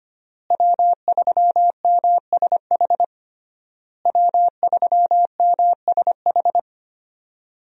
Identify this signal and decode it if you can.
Morse code, heard as a keyed tone, about 25 wpm, 700 Hz. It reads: W3MSH W3MSH
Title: USA Call Signs - 5 Characters - Repeated 3x - 25wpm